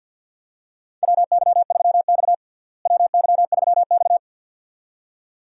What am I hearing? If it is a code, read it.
Morse code, heard as a keyed tone, about 46 wpm, 700 Hz. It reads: WY3X WY3X
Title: USA Call Signs - 4 Characters - Repeated 2x - 50wpm